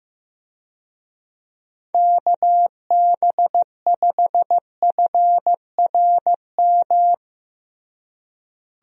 Morse code, heard as a keyed tone, about 16 wpm, 700 Hz. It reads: KB5FRM
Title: USA Call Signs - 6 Characters - Repeated 2x - 15wpm